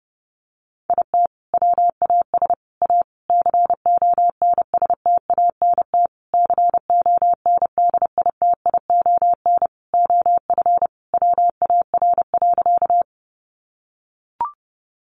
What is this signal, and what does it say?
Morse code, heard as a keyed tone, about 30 wpm, 700 Hz. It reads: IT WAS A CONSTANT CONDITION OF WAR.
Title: Sentences from Top 1500 Words 30wpm